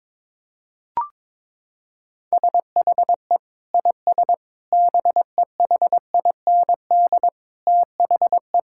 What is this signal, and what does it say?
Morse code, heard as a keyed tone, about 22 wpm, 700 Hz. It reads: SHE IS BEHIND THE
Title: Sentences from Top 1600 Words 22wpm